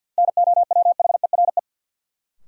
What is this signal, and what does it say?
Morse code, heard as a keyed tone, about 53 wpm, 700 Hz. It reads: NOWHERE
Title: Sets of 5 Words Encore - New Word Drill - 50wpm